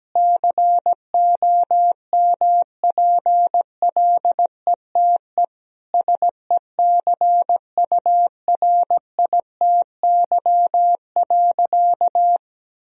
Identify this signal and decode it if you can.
Morse code, heard as a keyed tone, about 17 wpm, 700 Hz. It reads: COMPLETE SECURITY.
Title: Sentences from Top 1000 Words - Review Part 4 - 17wpm